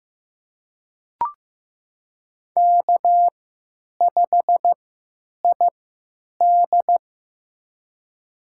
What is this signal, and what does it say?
Morse code, heard as a keyed tone, about 15 wpm, spaced as for 9 wpm, 700 Hz. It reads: K5ID
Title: USA Call Signs - 4 Characters - 3x Character Spacing - 15wpm